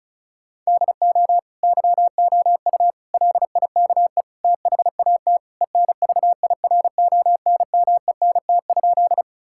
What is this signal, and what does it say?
Morse code, heard as a keyed tone, about 34 wpm, 700 Hz. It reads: DO YOU LIKE THAT ENVIRONMENT?
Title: Sentences from Top 1000 Words - Review Part 5 - 35wpm